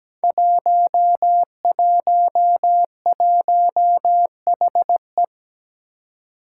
Morse code, heard as a keyed tone, about 17 wpm, 700 Hz. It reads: 111HE
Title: Five Challenging Letter-Number Combinations 17wpm